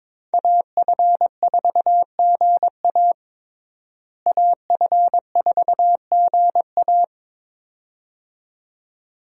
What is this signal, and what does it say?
Morse code, heard as a keyed tone, about 22 wpm, 700 Hz. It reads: AF4GA AF4GA
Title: USA Call Signs - 5 Characters - Repeated 2x - 22wpm